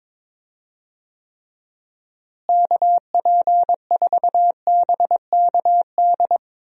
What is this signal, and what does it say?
Morse code, heard as a keyed tone, about 22 wpm, 700 Hz. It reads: KP4BKD
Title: DXCC-100 Call Signs - 6 Characters - Repeated 3x - 22wpm